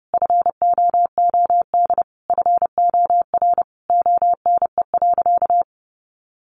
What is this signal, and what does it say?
Morse code, heard as a keyed tone, about 30 wpm, 700 Hz. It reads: FOOD FOR ONE.
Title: Sentences from Top 300 Words 30wpm